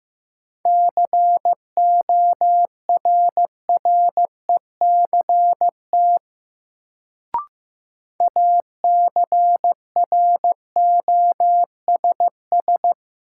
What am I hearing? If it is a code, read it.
Morse code, heard as a keyed tone, about 15 wpm, 700 Hz. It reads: CORRECT ACROSS
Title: Sentences from Top 700 Words - New Word Drill - 15wpm